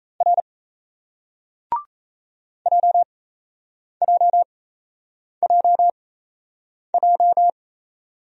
Morse code, heard as a keyed tone, about 37 wpm, spaced as for 8 wpm, 700 Hz. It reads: R JJJJ